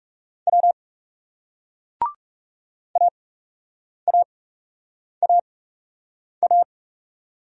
Morse code, heard as a keyed tone, about 40 wpm, spaced as for 8 wpm, 700 Hz. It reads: W AAAA